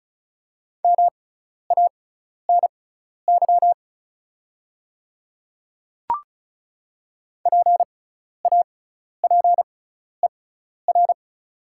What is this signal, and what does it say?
Morse code, heard as a keyed tone, about 36 wpm, spaced as for 12 wpm, 690 Hz. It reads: MANY PAPER
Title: 6x Character Spacing - Top 300 Words - 35wpm